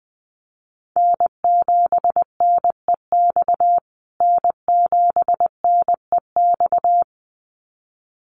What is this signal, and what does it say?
Morse code, heard as a keyed tone, about 20 wpm, 700 Hz. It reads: N7NEX N7NEX
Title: USA Call Signs Repeated 2x 20wpm